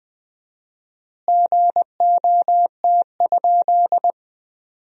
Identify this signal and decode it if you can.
Morse code, heard as a keyed tone, about 20 wpm, 700 Hz. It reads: GOT?
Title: 4x Word-Spacing - Sentences from Top 300 Words - 20wpm